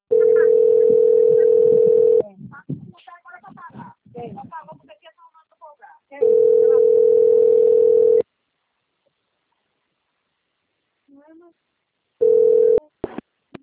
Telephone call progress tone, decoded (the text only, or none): ringback tone